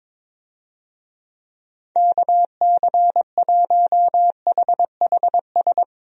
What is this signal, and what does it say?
Morse code, heard as a keyed tone, about 22 wpm, 700 Hz. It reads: KC1HHS
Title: USA Call Signs - 6 Characters - Repeated 3x - 22wpm